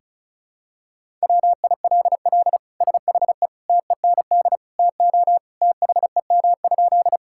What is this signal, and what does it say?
Morse code, heard as a keyed tone, about 36 wpm, 700 Hz. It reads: WILL SHE TEND TO THEM?
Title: Sentences from Top 1000 Words - Review Part 5 - 35wpm